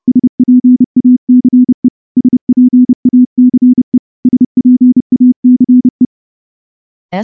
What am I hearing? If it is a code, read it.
Morse code, heard as a keyed tone, about 28 wpm, 250 Hz. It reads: SPACE SPACE SPACE
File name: morse-training-30WPM